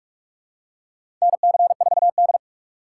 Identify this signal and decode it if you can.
Morse code, heard as a keyed tone, about 41 wpm, 700 Hz. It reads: NC4D
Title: USA Call Signs - 4 Characters - 2x Character Spacing - 45wpm